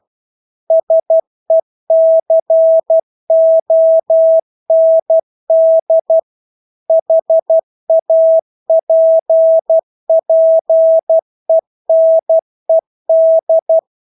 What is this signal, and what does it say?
Morse code, heard as a keyed tone, about 12 wpm, 650 Hz. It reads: SECOND HAPPENED
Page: Engelske ord 12 wpm